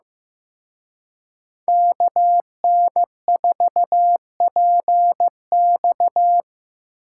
Morse code, heard as a keyed tone, about 17 wpm, 700 Hz. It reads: KN4PX
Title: Speed Racing - USA Call Signs 15wpm